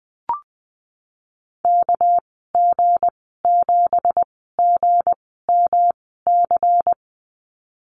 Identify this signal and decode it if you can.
Morse code, heard as a keyed tone, about 19 wpm, spaced as for 14 wpm, 700 Hz. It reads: KG7GMC
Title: 238 Lesson 38g - Letter q - Call Signs 20wpm